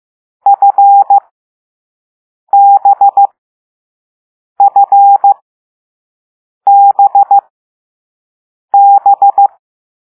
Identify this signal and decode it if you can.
Morse code, heard as a keyed tone, about 15 wpm, spaced as for 5 wpm, 800 Hz. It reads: FBFBB
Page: Play Lesson 6 C - all letters in previous lessons